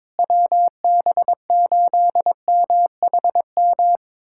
Morse code, heard as a keyed tone, about 22 wpm, 700 Hz. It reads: WB8MHM
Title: USA Callsign DE Callsign - 6 Characters - 22wpm